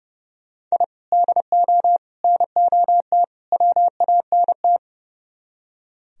Morse code, heard as a keyed tone, about 38 wpm, 700 Hz. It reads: I DO NOT WANT